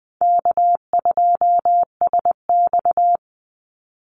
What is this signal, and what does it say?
Morse code, heard as a keyed tone, about 20 wpm, 700 Hz. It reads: K2SX
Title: Call Signs Repeated 3x - 20wpm